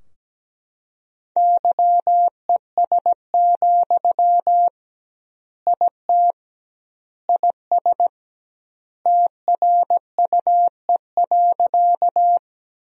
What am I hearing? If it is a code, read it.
Morse code, heard as a keyed tone, about 17 wpm, 700 Hz. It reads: YES, IT IS TRUE.
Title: 2x Word-Spacing - Sentences from Top 500 Words - 17wpm